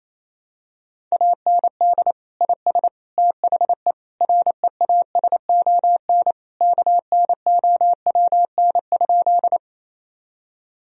Morse code, heard as a keyed tone, about 28 wpm, 700 Hz. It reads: AND IS THE REASON KNOWN?
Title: Sentences from Top 1000 Words - Review Part 3 - 28wpm